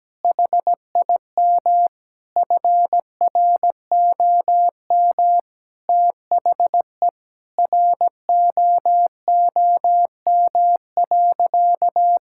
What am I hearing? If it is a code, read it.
Morse code, heard as a keyed tone, about 17 wpm, 700 Hz. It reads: HIM FROM THE ROOM.
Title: Sentences from Top 1000 Words - Review Part 5 - 17wpm